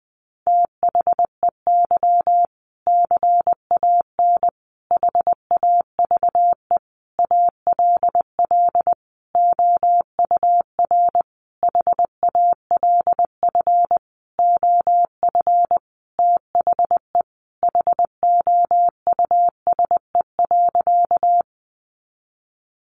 Morse code, heard as a keyed tone, about 20 wpm, 700 Hz. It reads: THEY CAN HAVE ALL OUR HALF OF THE HOUSE.
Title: Sentences from Top 400 Words 20wpm